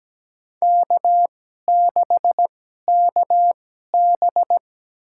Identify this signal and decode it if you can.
Morse code, heard as a keyed tone, about 17 wpm, spaced as for 12 wpm, 700 Hz. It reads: K6KB
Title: USA Call Signs - 2x Character Spacing - 17wpm